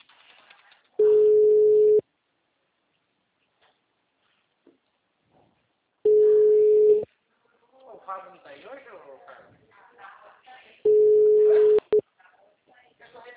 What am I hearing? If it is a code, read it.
Ringback tone (Europe, 425 Hz), with speech.
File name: out- -2014- - - .20590